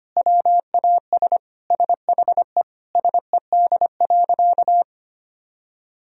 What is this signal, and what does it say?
Morse code, heard as a keyed tone, about 25 wpm, 700 Hz. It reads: WAS SHE SED.
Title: Sentences from Top 400 Words and Abbreviations 25wpm